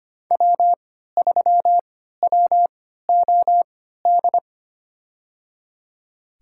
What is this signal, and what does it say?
Morse code, heard as a keyed tone, about 25 wpm, spaced as for 14 wpm, 700 Hz. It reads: W3WOD
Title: USA Call Signs - 5 Characters - 3x Character Spacing - 25wpm